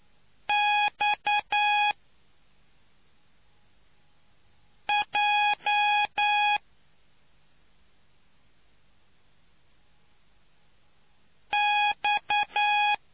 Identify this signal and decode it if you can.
Morse code, heard as a keyed tone, about 9 wpm, spaced as for 3 wpm, 830 Hz. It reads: XJ X